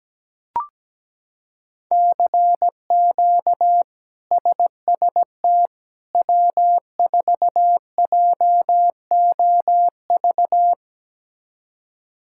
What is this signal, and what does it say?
Morse code, heard as a keyed tone, about 17 wpm, 700 Hz. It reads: CQ SST W4JOV
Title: K1USN SST QSO Practice - 17wpm